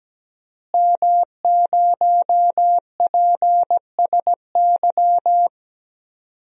Morse code, heard as a keyed tone, about 17 wpm, 700 Hz. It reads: M0PSY
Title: DXCC-100 Call Signs - 5 Characters - Repeated 2x - 17wpm